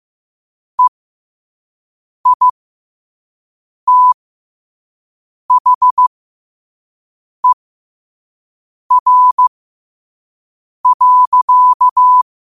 Morse code, heard as a keyed tone, about 15 wpm, spaced as for 5 wpm, 1000 Hz. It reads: EITHER.
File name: qotdfiles / 5wpm